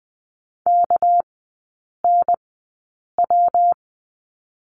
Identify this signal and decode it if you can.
Morse code, heard as a keyed tone, about 20 wpm, spaced as for 8 wpm, 700 Hz. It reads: KNW
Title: 198 Lesson 32e - Letter k - Three Characters 20wpm